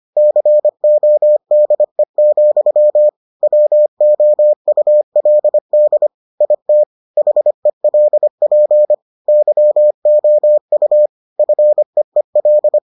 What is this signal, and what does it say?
Morse code, heard as a keyed tone, about 25 wpm, 600 Hz. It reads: CODE, WOULD IT HELP YOU FEEL